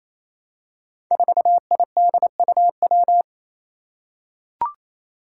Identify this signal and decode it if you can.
Morse code, heard as a keyed tone, about 28 wpm, 700 Hz. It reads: 4IDUW
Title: Five Challenging Letter-Number Combinations 28wpm